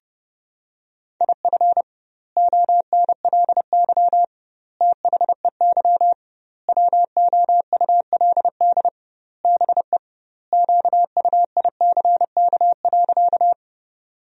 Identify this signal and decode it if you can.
Morse code, heard as a keyed tone, about 32 wpm, 700 Hz. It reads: IF ONLY THEY WOULD BE QUICK.
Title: 2x Word-Spacing - Sentences from Top 500 Words - 30wpm